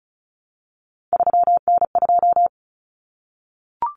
Morse code, heard as a keyed tone, about 34 wpm, 700 Hz. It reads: 3N2
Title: Common Cut Numbers - 3-Digit 35wpm